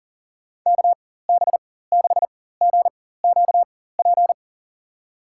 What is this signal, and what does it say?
Morse code, heard as a keyed tone, about 37 wpm, spaced as for 18 wpm, 700 Hz. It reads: KB6GQP
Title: USA Call Signs - 6 Characters - 4x Character Spacing - 40wpm